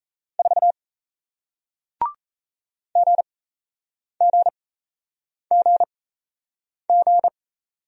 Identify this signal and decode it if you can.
Morse code, heard as a keyed tone, about 37 wpm, spaced as for 8 wpm, 700 Hz. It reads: 4 GGGG